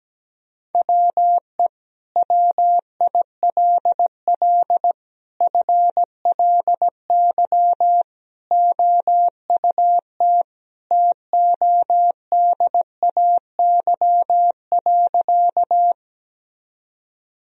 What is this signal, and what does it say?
Morse code, heard as a keyed tone, about 17 wpm, 700 Hz. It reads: WE WILL FLY OUT TODAY.
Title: Sentences from Top 1000 Words - Review Part 2 - 17wpm